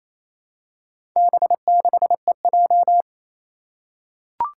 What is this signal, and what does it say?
Morse code, heard as a keyed tone, about 28 wpm, 700 Hz. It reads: B6EJ
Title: Four Challenging Letter-Number Combinations 28wpm